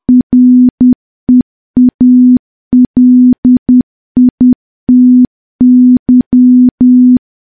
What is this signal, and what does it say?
Morse code, heard as a keyed tone, about 10 wpm, 250 Hz. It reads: REALITY
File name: morse / morse-training-10WPM